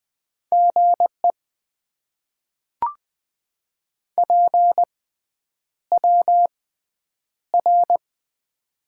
Morse code, heard as a keyed tone, about 20 wpm, spaced as for 9 wpm, 700 Hz. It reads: GE PWR
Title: Top 90 QSO Elements - 6x Character Spacing - 20wpm